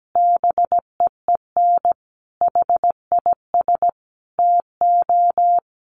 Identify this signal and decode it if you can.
Morse code, heard as a keyed tone, about 17 wpm, 700 Hz. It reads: BEEN HIS TO
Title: Sentences from Top 1500 Words 17wpm